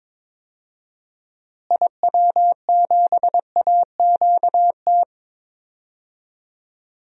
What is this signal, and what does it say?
Morse code, heard as a keyed tone, about 22 wpm, 700 Hz. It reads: IW7AQT
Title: DXCC-100 Call Signs - 6 Characters - Repeated 2x - 22wpm